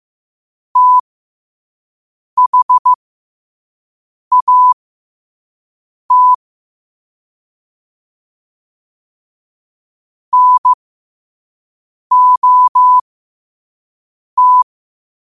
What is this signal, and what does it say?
Morse code, heard as a keyed tone, about 15 wpm, spaced as for 5 wpm, 1000 Hz. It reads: THAT NOT
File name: qotdfiles / 5wpm